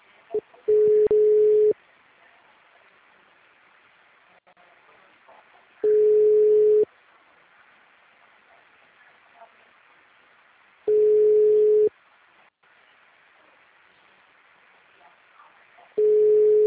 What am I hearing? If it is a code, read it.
Ringback tone (Europe, 425 Hz).